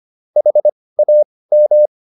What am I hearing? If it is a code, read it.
Morse code, heard as a keyed tone, about 25 wpm, spaced as for 18 wpm, 600 Hz. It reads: HAM